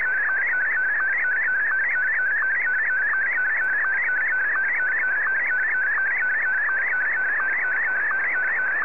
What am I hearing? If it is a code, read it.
SSTV (slow-scan television), PD 90.
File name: sstv / W2180-VK8VTX